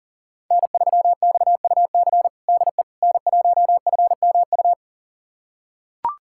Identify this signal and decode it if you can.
Morse code, heard as a keyed tone, about 40 wpm, 700 Hz. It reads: N3XUC DE N1FMU
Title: USA Callsign DE Callsign - 5 Characters - 40wpm